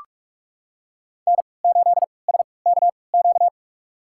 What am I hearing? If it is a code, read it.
Morse code, heard as a keyed tone, about 45 wpm, spaced as for 25 wpm, 700 Hz. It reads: N8SKQ
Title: USA Call Signs - 5 Characters - 3x Character Spacing - 45wpm